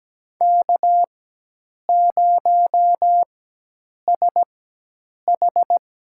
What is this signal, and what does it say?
Morse code, heard as a keyed tone, about 17 wpm, spaced as for 8 wpm, 700 Hz. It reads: K0SH
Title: USA Call Signs - 4 Characters - 4x Character Spacing - 17wpm